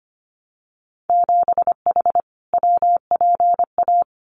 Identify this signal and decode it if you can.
Morse code, heard as a keyed tone, about 25 wpm, 700 Hz. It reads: 7H WPA
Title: Winter Field Day Exchanges Encore 25wpm